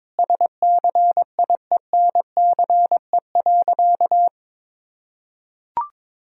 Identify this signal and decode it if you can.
Morse code, heard as a keyed tone, about 23 wpm, 700 Hz. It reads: SCIENCE.
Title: Sentences from Top 1000 Words - Review Part 2 - 22wpm